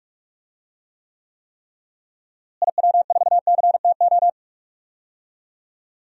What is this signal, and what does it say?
Morse code, heard as a keyed tone, about 45 wpm, 700 Hz. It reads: IW4CTO
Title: DXCC-100 Call Signs - 6 Characters - Repeated 3x - 45wpm